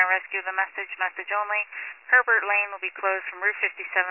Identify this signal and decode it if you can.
Emergency Alert System (SAME) tones, heard as harsh buzzing bursts, with speech. It is heard from 0.7 to 3.7 s.